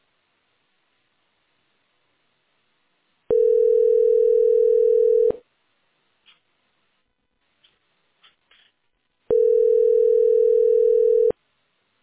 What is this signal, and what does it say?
Ringback tone (North America), 440 + 480 Hz.